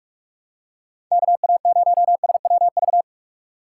Morse code, heard as a keyed tone, about 45 wpm, 700 Hz. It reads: KA0SWV